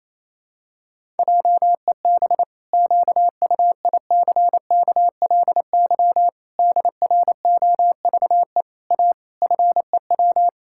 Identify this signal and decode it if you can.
Morse code, heard as a keyed tone, about 27 wpm, 700 Hz. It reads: JEB QUICKLY DROVE A FEW
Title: Pangram Challenge v2 - 28wpm